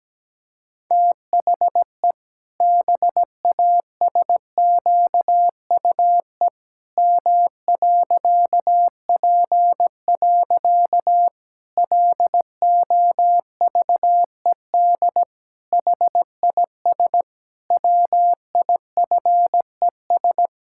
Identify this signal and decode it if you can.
Morse code, heard as a keyed tone, about 17 wpm, 700 Hz. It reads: THE BASQUE M.P. LOVED HIS WIFES
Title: Pangram Challenge v2 - 17wpm